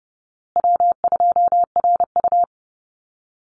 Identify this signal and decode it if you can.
Morse code, heard as a keyed tone, about 30 wpm, 700 Hz. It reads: W2RU
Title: Easy Call Signs Repeated 3x - 30wpm